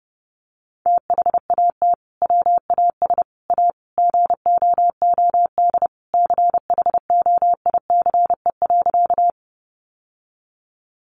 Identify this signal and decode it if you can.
Morse code, heard as a keyed tone, about 30 wpm, 700 Hz. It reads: THAT WAS A GOOD CHOICE.
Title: Sentences from Top 900 Words 30wpm